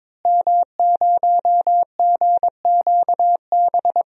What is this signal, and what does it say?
Morse code, heard as a keyed tone, about 22 wpm, 700 Hz. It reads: M0GQB
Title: DXCC-100 Call Signs - 5 Characters - Repeated 3x - 22wpm